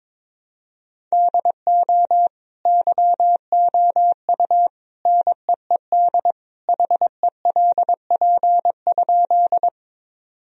Morse code, heard as a keyed tone, about 22 wpm, 700 Hz. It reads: DO YOU NEED HELP?
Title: Sentences from Top 1000 Words - Review Part 1 - 22wpm